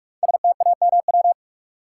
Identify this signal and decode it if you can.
Morse code, heard as a keyed tone, about 44 wpm, 700 Hz. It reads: STAMW
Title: Code Groups - 5 Letters - 45wpm